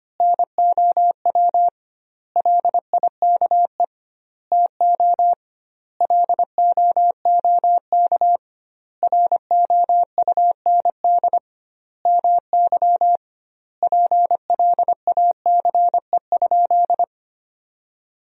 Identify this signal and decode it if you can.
Morse code, heard as a keyed tone, about 25 wpm, 700 Hz. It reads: NOW LIKE TO LOOK ROUND MY PLACE?
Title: 2x Word-Spacing - Sentences from Top 200 Words - 25wpm